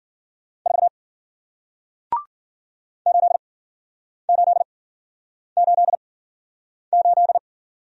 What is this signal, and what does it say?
Morse code, heard as a keyed tone, about 54 wpm, spaced as for 9 wpm, 700 Hz. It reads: 4 8888